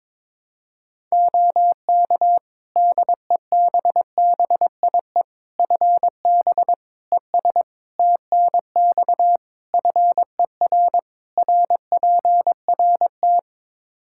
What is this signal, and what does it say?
Morse code, heard as a keyed tone, about 21 wpm, 700 Hz. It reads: OK DEBBIE FB ES TNX FER RPRT
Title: Practice QSO - Exchange 2 - Info - 22wpm